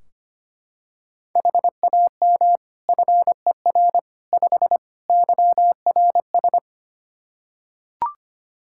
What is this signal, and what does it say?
Morse code, heard as a keyed tone, about 25 wpm, 700 Hz. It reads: HAM FER 5 YRS
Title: K1USN SST QSO Practice 25wpm